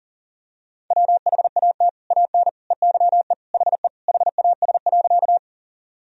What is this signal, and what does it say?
Morse code, heard as a keyed tone, about 40 wpm, 700 Hz. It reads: WHAT AN EYE HE HAS.
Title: Sentences from Top 1000 Words - Review Part 1 - 40wpm